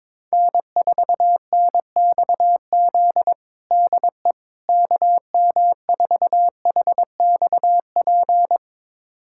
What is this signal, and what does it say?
Morse code, heard as a keyed tone, about 22 wpm, 700 Hz. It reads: N4NXZ DE KM4HXP
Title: USA Callsign DE Callsign 22wpm